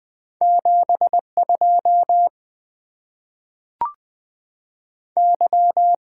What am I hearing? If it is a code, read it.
Morse code, heard as a keyed tone, about 20 wpm, 690 Hz. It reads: 7 2 Y
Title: Two Letter-Number Combinations - 20wpm